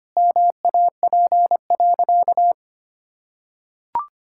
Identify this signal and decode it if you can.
Morse code, heard as a keyed tone, about 25 wpm, 700 Hz. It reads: MAP.
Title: Sentences from Top 1000 Words - Review Part 6 - 25wpm